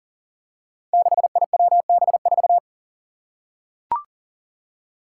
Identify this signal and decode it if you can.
Morse code, heard as a keyed tone, about 40 wpm, 700 Hz. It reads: 6IWB4
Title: Five Challenging Letter-Number Combinations 40wpm